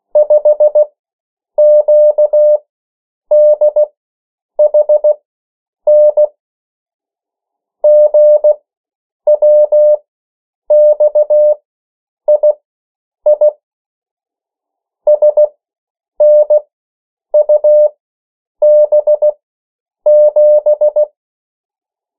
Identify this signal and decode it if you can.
Morse code, heard as a keyed tone, about 16 wpm, spaced as for 8 wpm, 600 Hz. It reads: 5QDHN GWXII SNUB7
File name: morse mp3 / 14 urok 13